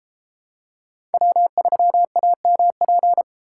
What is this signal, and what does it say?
Morse code, heard as a keyed tone, about 31 wpm, 700 Hz. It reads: W3AMP